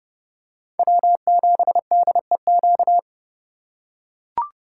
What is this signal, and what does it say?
Morse code, heard as a keyed tone, about 30 wpm, 700 Hz. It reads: W7DEQ
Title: USA Call Signs - 5 Characters - 3x Character Spacing - 30wpm